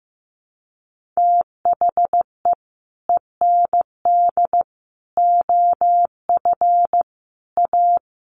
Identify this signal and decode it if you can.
Morse code, heard as a keyed tone, about 15 wpm, 700 Hz. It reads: THE END OF A
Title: Sets of 5 Words Encore2 15wpm